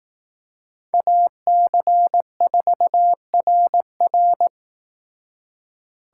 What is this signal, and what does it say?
Morse code, heard as a keyed tone, about 18 wpm, 700 Hz. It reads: AC4RR
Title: Speed Racing - USA Call Signs 15wpm